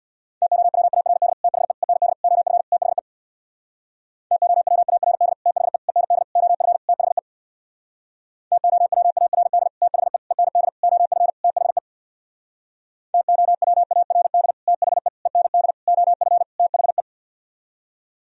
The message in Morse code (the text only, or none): TOWARD THE END OF THE TOWARD THE END OF THE TOWARD THE END OF THE TOWARD THE END OF THE